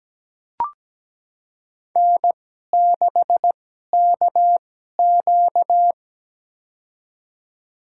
Morse code, heard as a keyed tone, about 17 wpm, spaced as for 12 wpm, 700 Hz. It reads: N6KQ
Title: USA Call Signs - 4 Characters - 2x Character Spacing - 17wpm